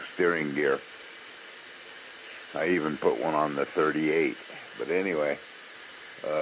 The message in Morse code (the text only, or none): none